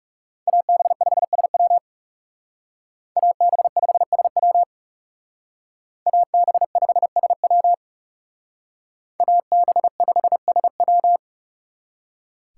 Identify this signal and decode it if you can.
Morse code, heard as a keyed tone, about 35 wpm, 700 Hz. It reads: AB5SW AB5SW AB5SW AB5SW
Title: Speed Racing - USA 2x2 Call Signs 30wpm